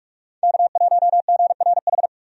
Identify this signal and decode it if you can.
Morse code, heard as a keyed tone, about 46 wpm, 700 Hz. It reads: K1GRH
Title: USA Call Signs - 5 Characters - 45wpm